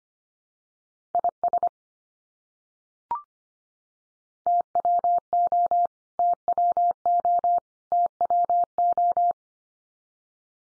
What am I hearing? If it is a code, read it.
Morse code, heard as a keyed tone, about 25 wpm, 700 Hz. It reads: IS TWO TWO TWO
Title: Top 100 Words 25wpm x3